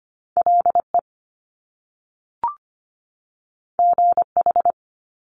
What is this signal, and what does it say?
Morse code, heard as a keyed tone, about 25 wpm, 700 Hz. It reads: LE GH
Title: Top Two Letter Combinations in Words 25wpm